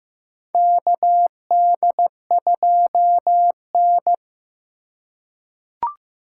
Morse code, heard as a keyed tone, about 15 wpm, 700 Hz. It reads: KD2N
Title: USA Call Signs - 4 Characters - 2x Character Spacing - 15wpm